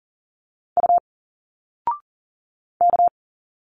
Morse code, heard as a keyed tone, about 40 wpm, 700 Hz. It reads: U K
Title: Single Letters 40wpm